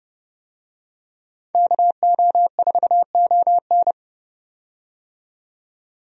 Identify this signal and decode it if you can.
Morse code, heard as a keyed tone, about 30 wpm, 700 Hz. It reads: KO4ON
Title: USA Call Signs - 5 Characters - 30wpm